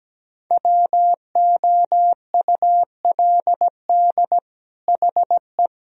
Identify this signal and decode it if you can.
Morse code, heard as a keyed tone, about 17 wpm, 700 Hz. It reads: WOULD HE
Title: Sentences from Top 1000 Words - Review Part 1 - 17wpm